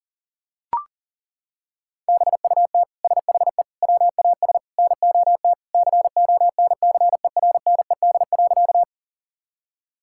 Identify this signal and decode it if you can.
Morse code, heard as a keyed tone, about 40 wpm, 700 Hz. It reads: BUT SHE WAS NOT CONCERNED.
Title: Sentences from Top 1000 Words - Review Part 6 - 40wpm